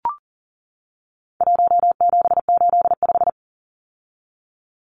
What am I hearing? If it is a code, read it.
Morse code, heard as a keyed tone, about 40 wpm, 700 Hz. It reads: 1785
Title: Numbers - 4-Digit 40wpm